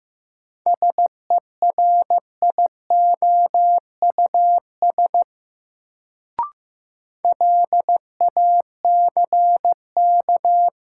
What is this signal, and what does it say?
Morse code, heard as a keyed tone, about 15 wpm, 700 Hz. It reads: SERIOUS LACK
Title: Sentences from Top 1100 Words - New Word Drill Alt - 15wpm